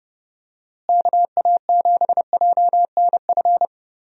Morse code, heard as a keyed tone, about 30 wpm, 700 Hz. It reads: KA7JNF
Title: USA Call Signs - 6 Characters - Repeated 2x - 30wpm